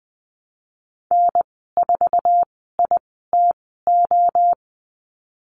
Morse code, heard as a keyed tone, about 20 wpm, spaced as for 15 wpm, 700 Hz. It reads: N4ITO
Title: 048 Lesson 9g - Number 4 - Call Signs 20wpm